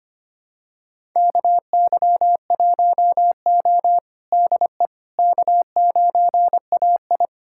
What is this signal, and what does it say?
Morse code, heard as a keyed tone, about 25 wpm, 700 Hz. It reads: KY1O DE K9AI
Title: USA Callsign DE Callsign - 4 Characters - 25wpm